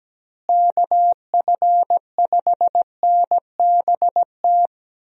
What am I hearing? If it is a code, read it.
Morse code, heard as a keyed tone, about 17 wpm, 700 Hz. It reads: KF5NBT
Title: USA Call Signs - 6 Characters - Repeated 3x - 17wpm